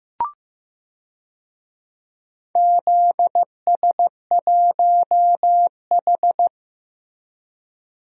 Morse code, heard as a keyed tone, about 15 wpm, 700 Hz. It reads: ZS1H
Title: DXCC-100 Call Signs - 4 Characters - Repeated 3x - 15wpm